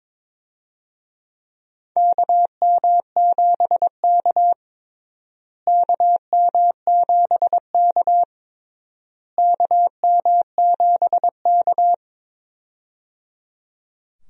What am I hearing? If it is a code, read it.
Morse code, heard as a keyed tone, about 22 wpm, 700 Hz. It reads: KM7K KM7K KM7K
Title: USA Call Signs - 4 Characters - Repeated 3x - 22wpm